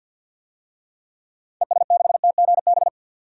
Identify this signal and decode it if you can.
Morse code, heard as a keyed tone, about 46 wpm, 700 Hz. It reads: ES6TGB
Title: DXCC-100 Call Signs - 6 Characters - 50wpm